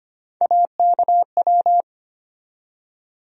Morse code, heard as a keyed tone, about 25 wpm, 700 Hz. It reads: AKW
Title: USA Call Sign Postfixes - 3 Letters 25wpm